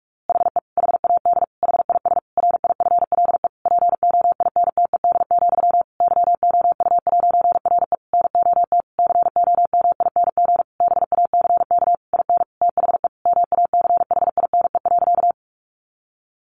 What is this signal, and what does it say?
Morse code, heard as a keyed tone, about 44 wpm, 700 Hz. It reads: HE HAD HIS RIFLE POINTED, YOU'RE NOT COMING BACK IN THE MACHINE.
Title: A Sound Of Thunder 45wpm